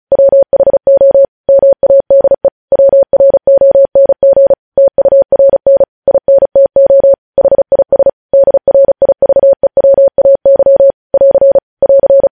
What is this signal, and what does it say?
Morse code, heard as a keyed tone, about 36 wpm, 550 Hz. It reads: WHO MADE WRONG TURN INTO HIS DRIVEWAY <AR> <AR>
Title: News Headlines at 35